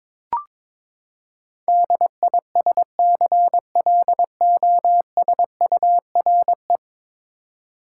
Morse code, heard as a keyed tone, about 22 wpm, 700 Hz. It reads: DISCLOSURE
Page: morse-code-ninja-practice-22wpm